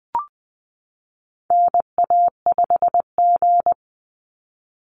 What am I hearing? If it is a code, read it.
Morse code, heard as a keyed tone, about 20 wpm, 700 Hz. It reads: NA5G
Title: Call Signs Repeated 3x - 20wpm